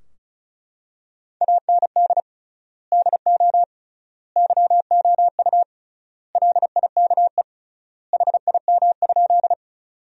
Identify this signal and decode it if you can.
Morse code, heard as a keyed tone, about 36 wpm, 700 Hz. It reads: AND DO YOU LIKE HIM?
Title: 3x Word-Spacing - Sentences from Top 100 Words - 35wpm